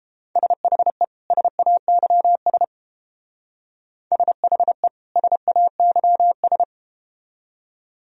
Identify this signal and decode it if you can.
Morse code, heard as a keyed tone, about 31 wpm, 700 Hz. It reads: SHE SAYS SHE SAYS